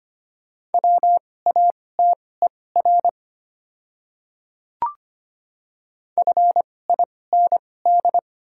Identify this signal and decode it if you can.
Morse code, heard as a keyed tone, about 25 wpm, spaced as for 18 wpm, 690 Hz. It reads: WATER FIND
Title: 2x Character Spacing - Top 100 Words - 25wpm